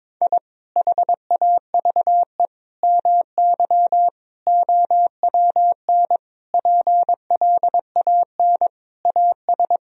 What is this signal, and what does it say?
Morse code, heard as a keyed tone, about 21 wpm, 700 Hz. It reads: I HAVE MY OWN PLAN AS
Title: Sentences from Top 1000 Words - Review Part 2 - 22wpm